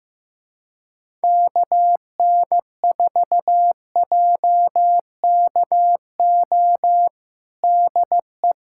Morse code, heard as a keyed tone, about 15 wpm, 700 Hz. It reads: KN4JKO DE
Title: USA Callsign DE Callsign 15wpm